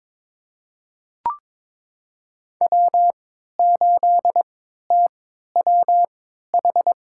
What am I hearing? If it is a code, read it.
Morse code, heard as a keyed tone, about 22 wpm, spaced as for 13 wpm, 700 Hz. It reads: W8TWH
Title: USA Call Signs - 5 Characters - 3x Character Spacing - 22wpm